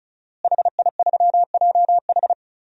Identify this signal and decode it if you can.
Morse code, heard as a keyed tone, about 34 wpm, 700 Hz. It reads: HI3JH
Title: DXCC-100 Call Signs - 5 Characters - Repeated 2x - 35wpm